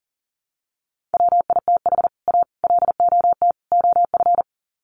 Morse code, heard as a keyed tone, about 40 wpm, 700 Hz. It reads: WITH A LOT OF
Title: Sets of 4 Words 40wpm